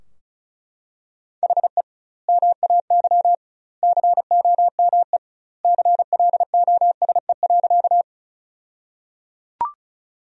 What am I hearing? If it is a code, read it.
Morse code, heard as a keyed tone, about 36 wpm, 700 Hz. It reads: HE MAY COME CLOSE.
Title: 2x Word-Spacing - Sentences from Top 300 Words - 35wpm